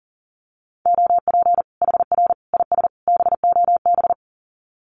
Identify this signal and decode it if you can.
Morse code, heard as a keyed tone, about 40 wpm, 700 Hz. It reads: OP HR IS BOB
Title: Common QSO Phrases 40wpm